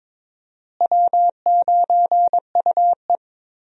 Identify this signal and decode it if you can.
Morse code, heard as a keyed tone, about 21 wpm, 700 Hz. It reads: W9UE
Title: USA Call Signs - 4 Characters - Repeated 3x - 22wpm